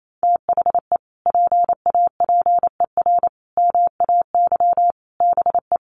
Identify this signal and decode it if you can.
Morse code, heard as a keyed tone, about 28 wpm, 700 Hz. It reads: THE PAPER MAY BE
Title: Sentences from Top 1400 Words 28wpm